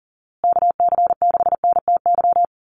Morse code, heard as a keyed tone, about 40 wpm, 700 Hz. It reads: KC6NTY
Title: USA Call Signs Repeated 2x 40wpm